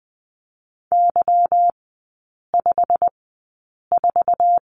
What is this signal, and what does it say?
Morse code, heard as a keyed tone, about 20 wpm, spaced as for 8 wpm, 700 Hz. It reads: Y54
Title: 198 Lesson 32e - Letter k - Three Characters 20wpm